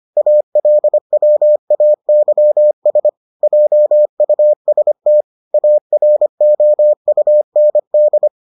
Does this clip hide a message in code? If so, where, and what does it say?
Yes: Morse code from 0.2 to 8.3 s, heard as a keyed tone, about 25 wpm, 600 Hz. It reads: ALWAYS JUST AROUND